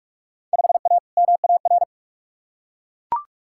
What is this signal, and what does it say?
Morse code, heard as a keyed tone, about 41 wpm, 700 Hz. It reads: 5A MAR
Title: Field Day Exchanges Encore - 45wpm